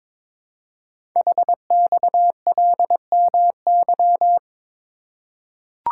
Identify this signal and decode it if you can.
Morse code, heard as a keyed tone, about 22 wpm, 700 Hz. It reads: HXLMY
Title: Code Groups - 5 Letters - 22wpm